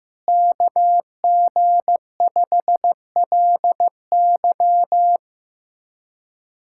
Morse code, heard as a keyed tone, about 15 wpm, 700 Hz. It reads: KG5LY
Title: USA Callsign DE Callsign - 5 Characters - 15wpm